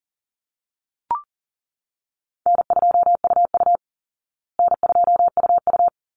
Morse code, heard as a keyed tone, about 40 wpm, 700 Hz. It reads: N2UU N2UU
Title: Easy Call Signs Repeated 2x - 40wpm